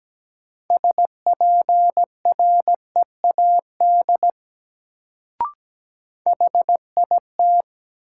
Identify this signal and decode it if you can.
Morse code, heard as a keyed tone, about 17 wpm, 700 Hz. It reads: SPREAD HIT
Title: Sentences from Top 1200 Words - New Word Drill - 17wpm